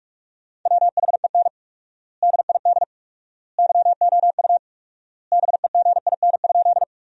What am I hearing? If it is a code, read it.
Morse code, heard as a keyed tone, about 46 wpm, 700 Hz. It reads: WHEN DID YOU BEGIN?
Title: 4x Word-Spacing - Sentences from Top 300 Words - 45wpm